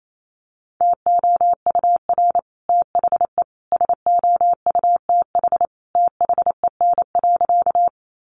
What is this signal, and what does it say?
Morse code, heard as a keyed tone, about 28 wpm, 700 Hz. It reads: TOUR THE SOUTH THEN.
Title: Sentences from Top 1400 Words 28wpm